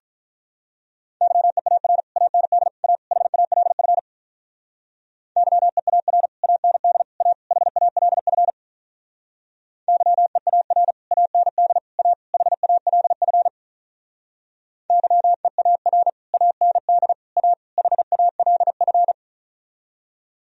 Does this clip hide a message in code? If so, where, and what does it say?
Yes: Morse code from 1.2 to 19.1 s, heard as a keyed tone, about 45 wpm, 700 Hz. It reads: YEAR AND A HALF YEAR AND A HALF YEAR AND A HALF YEAR AND A HALF